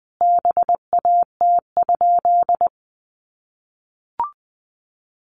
Morse code, heard as a keyed tone, about 20 wpm, 700 Hz. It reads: BAT?
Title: Sentences from Top 1500 Words 20wpm